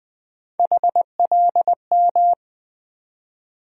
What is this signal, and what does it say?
Morse code, heard as a keyed tone, about 20 wpm, 700 Hz. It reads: HLM
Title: USA Call Sign Postfixes - 3 Letters 20wpm